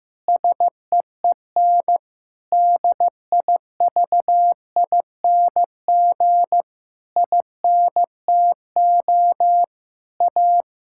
Morse code, heard as a keyed tone, about 15 wpm, 700 Hz. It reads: SEEN DIVING INTO A
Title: Pangram Challenge v2 - 15wpm